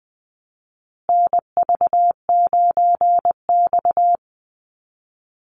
Morse code, heard as a keyed tone, about 20 wpm, 700 Hz. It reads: NV9X
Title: Call Signs Repeated 2x - 20wpm